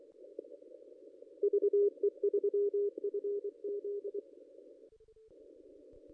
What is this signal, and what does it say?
Morse code, heard as a keyed tone, about 23 wpm, 440 Hz. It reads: VE3FZ
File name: contests / ss-cw-k5tr-op-n5rz -1800